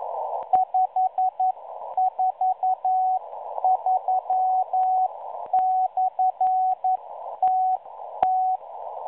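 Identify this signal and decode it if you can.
Morse code, heard as a keyed tone, about 11 wpm, spaced as for 9 wpm, 750 Hz. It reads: 543/TT
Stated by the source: Morse signal received on 5463.0 kHz via a WebSDR receiver, catalogued as numbers station M03.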